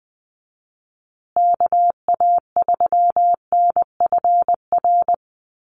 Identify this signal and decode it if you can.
Morse code, heard as a keyed tone, about 20 wpm, 700 Hz. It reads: KA3NFR
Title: USA Call Signs Repeated 3x 20wpm